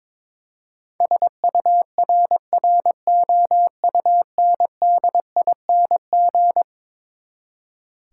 Morse code, heard as a keyed tone, about 22 wpm, 700 Hz. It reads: SURROUNDING
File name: practice / 22wpm / Eleven Letter Words - 22wpm